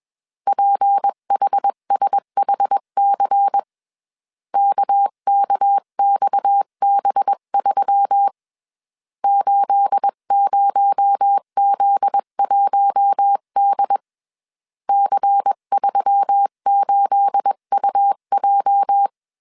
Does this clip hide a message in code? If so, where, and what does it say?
Yes: Morse code from 0.5 to 19.1 s, heard as a keyed tone, about 21 wpm, 800 Hz. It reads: PHSHC KKXB3 80Z1D C38UJ